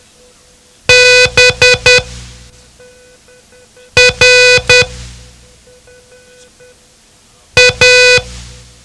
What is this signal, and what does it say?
Morse code, heard as a keyed tone, about 10 wpm, 500 Hz. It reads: B R A
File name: ham radio / CodeQuick / Code Quick 07